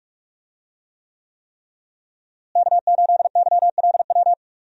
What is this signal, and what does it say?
Morse code, heard as a keyed tone, about 41 wpm, 700 Hz. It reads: K8YLW
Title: USA Call Signs - 5 Characters - Repeated 3x - 45wpm